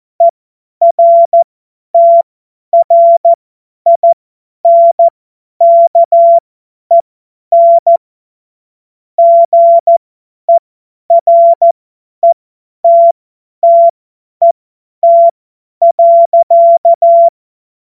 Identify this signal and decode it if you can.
Morse code, heard as a keyed tone, about 14 wpm, spaced as for 10 wpm, 680 Hz. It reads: ERTRINKEN GERETTET.
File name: Witz-01-Juli-2025-50-F70